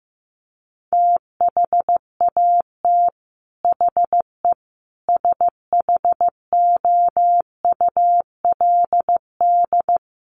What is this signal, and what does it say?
Morse code, heard as a keyed tone, about 15 wpm, 700 Hz. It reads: THAT HE SHOULD
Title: Sentences from Top 1500 Words 15wpm